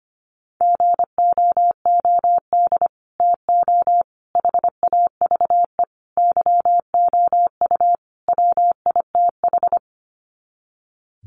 Morse code, heard as a keyed tone, about 25 wpm, 700 Hz. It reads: GOOD TO HAVE YOU WITH
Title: Sets of 5 Words Encore2 25wpm